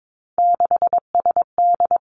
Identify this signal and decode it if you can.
Morse code, heard as a keyed tone, about 22 wpm, 700 Hz. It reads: 6SD
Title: Three Challenging Letter-Number Combinations 22wpm